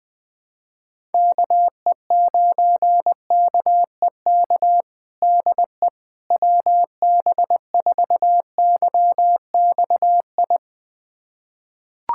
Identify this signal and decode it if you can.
Morse code, heard as a keyed tone, about 20 wpm, 700 Hz. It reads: KE9KEK DE WB4YXI
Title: USA Callsign DE Callsign - 6 Characters - 20wpm